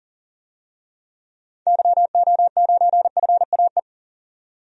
Morse code, heard as a keyed tone, about 40 wpm, 700 Hz. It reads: YO9FAE
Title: DXCC-100 Call Signs - 6 Characters - 40wpm